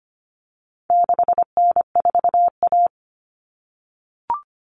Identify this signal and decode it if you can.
Morse code, heard as a keyed tone, about 25 wpm, 700 Hz. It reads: 6N4A
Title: Common Cut Numbers - 4-Digit 25wpm